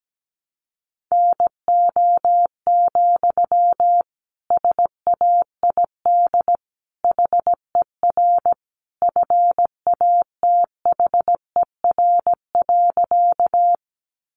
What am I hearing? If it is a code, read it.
Morse code, heard as a keyed tone, about 17 wpm, 700 Hz. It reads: NO, SAID HER FATHER.
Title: Sentences from Top 300 Words 17wpm